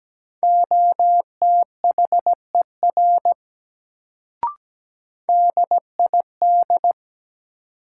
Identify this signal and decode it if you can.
Morse code, heard as a keyed tone, about 17 wpm, 700 Hz. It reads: OTHER DID
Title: Sentences from Top 100 Words - New Word Drill - 17wpm